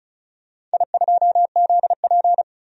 Morse code, heard as a keyed tone, about 36 wpm, 700 Hz. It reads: I2ZP
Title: DXCC-100 Call Signs - 4 Characters - 35wpm